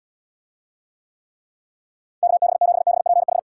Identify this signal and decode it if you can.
Morse code, heard as a keyed tone, about 76 wpm, 700 Hz. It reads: KB7BPH